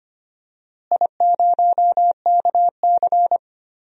Morse code, heard as a keyed tone, about 25 wpm, 700 Hz. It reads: I0KC
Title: DXCC-100 Call Signs - 4 Characters - Repeated 2x - 25wpm